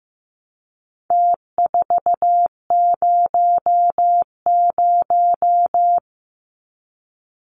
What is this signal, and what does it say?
Morse code, heard as a keyed tone, about 15 wpm, 700 Hz. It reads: T400
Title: Numbers - Contest Style - 4-Digit 15wpm